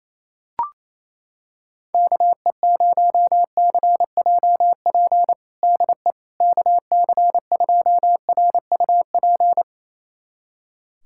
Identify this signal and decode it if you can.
Morse code, heard as a keyed tone, about 28 wpm, 700 Hz. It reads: KE0CJP DE KC2RUP
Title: USA Callsign DE Callsign - 6 Characters - 28wpm